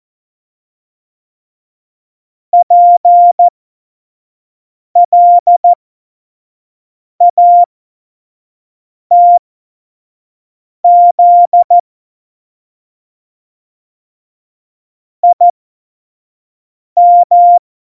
Morse code, heard as a keyed tone, about 14 wpm, spaced as for 5 wpm, 700 Hz. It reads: PLATZ IM
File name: Morsekurs / Witze / Witz 22. Dezember 2020, 25 F70